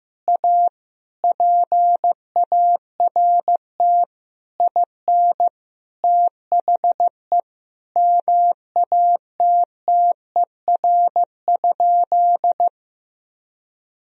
Morse code, heard as a keyed tone, about 16 wpm, 700 Hz. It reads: A PART IN THE MATTER?
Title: Sentences from Top 1000 Words - Review Part 1 - 15wpm